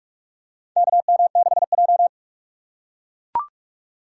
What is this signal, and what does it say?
Morse code, heard as a keyed tone, about 44 wpm, 700 Hz. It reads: KM6J
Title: USA Call Signs - 4 Characters - 6x Character Spacing - 45wpm